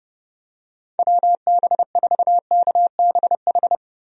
Morse code, heard as a keyed tone, about 31 wpm, 700 Hz. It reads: WB4KBH
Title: USA Call Signs - 6 Characters - 30wpm